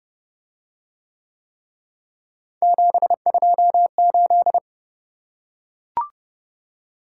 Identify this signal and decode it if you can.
Morse code, heard as a keyed tone, about 28 wpm, 700 Hz. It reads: 728